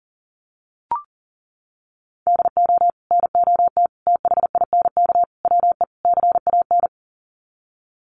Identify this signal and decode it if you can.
Morse code, heard as a keyed tone, about 40 wpm, 700 Hz. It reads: DO NOT THINK WE CAN
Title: Sets of 5 Words Encore2 40wpm